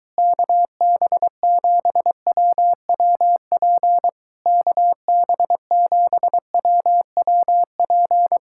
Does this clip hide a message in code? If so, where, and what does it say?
Yes: Morse code from 0.2 to 8.4 s, heard as a keyed tone, about 23 wpm, 700 Hz. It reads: KB7WWP KB7WWP